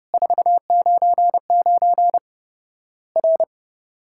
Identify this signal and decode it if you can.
Morse code, heard as a keyed tone, about 32 wpm, 700 Hz. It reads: 499 R
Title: Worked All Europe DX Contest - QTC Exchange - 30wpm